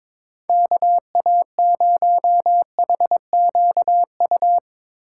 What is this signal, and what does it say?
Morse code, heard as a keyed tone, about 23 wpm, 700 Hz. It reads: KA0HQU
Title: Speed Racing - USA 2x3 Call Signs 22wpm